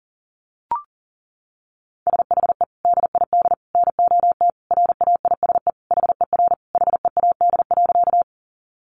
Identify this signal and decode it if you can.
Morse code, heard as a keyed tone, about 40 wpm, 700 Hz. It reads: SHE DID NOT RAISE HER HEAD.
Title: Sentences from Top 1400 Words 40wpm